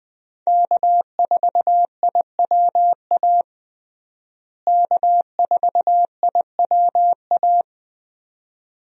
Morse code, heard as a keyed tone, about 20 wpm, 700 Hz. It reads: K4IWA K4IWA
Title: USA Call Signs - 5 Characters - Repeated 3x - 20wpm